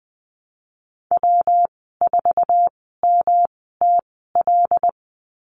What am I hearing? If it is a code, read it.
Morse code, heard as a keyed tone, about 20 wpm, spaced as for 15 wpm, 700 Hz. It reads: W4MTL
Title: 128 Lesson 21g - Number 6 - Call Signs 20wpm